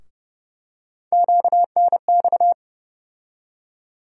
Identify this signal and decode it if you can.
Morse code, heard as a keyed tone, about 30 wpm, 700 Hz. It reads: QNX
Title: USA Call Sign Postfixes - 3 Letters 30wpm